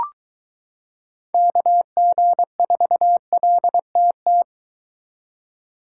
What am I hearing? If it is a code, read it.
Morse code, heard as a keyed tone, about 23 wpm, 700 Hz. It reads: KG4LTT